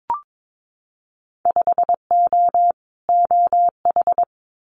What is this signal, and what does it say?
Morse code, heard as a keyed tone, about 22 wpm, 700 Hz. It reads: 5O OH
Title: Winter Field Day Exchanges Encore 22wpm